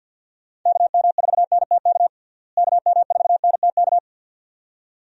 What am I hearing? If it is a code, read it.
Morse code, heard as a keyed tone, about 52 wpm, 700 Hz. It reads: KM4NTK KM4NTK
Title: USA Call Signs - 6 Characters - Repeated 2x - 50wpm